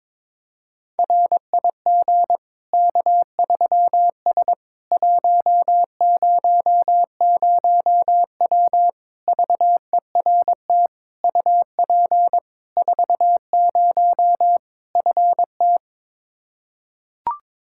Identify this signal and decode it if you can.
Morse code, heard as a keyed tone, about 22 wpm, 700 Hz. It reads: RIG K3S 100W VERT UP 40 FT
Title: K1USN SST QSO Practice 22wpm